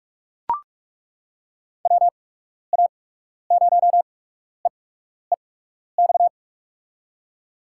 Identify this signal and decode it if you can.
Morse code, heard as a keyed tone, about 48 wpm, spaced as for 12 wpm, 700 Hz. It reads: WA0EEX
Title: USA Call Signs - 6 Characters - 8x Character Spacing - 45wpm